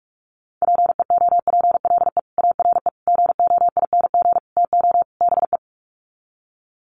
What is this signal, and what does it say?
Morse code, heard as a keyed tone, about 46 wpm, 700 Hz. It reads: PEOPLE ARE GOING TO BE
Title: Sets of 5 Words Encore 45wpm